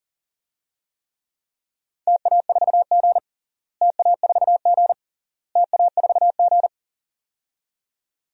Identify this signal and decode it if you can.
Morse code, heard as a keyed tone, about 40 wpm, 700 Hz. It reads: TA4G TA4G TA4G
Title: DXCC-100 Call Signs - 4 Characters - Repeated 3x - 40wpm